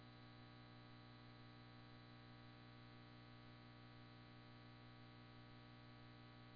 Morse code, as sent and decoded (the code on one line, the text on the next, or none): none